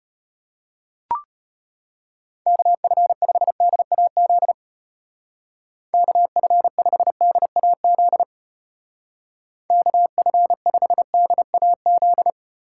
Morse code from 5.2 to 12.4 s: KF5DAZ KF5DAZ

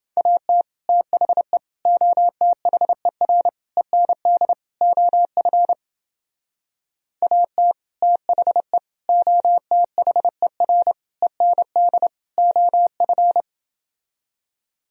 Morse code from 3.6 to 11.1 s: END OF AT THE OTHER

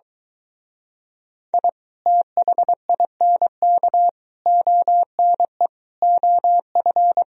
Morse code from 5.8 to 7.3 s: OF

